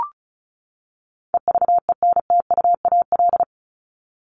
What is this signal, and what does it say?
Morse code, heard as a keyed tone, about 34 wpm, 700 Hz. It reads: EVENTUAL
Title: LY Suffix - Top 100 Words with Base Words 35wpm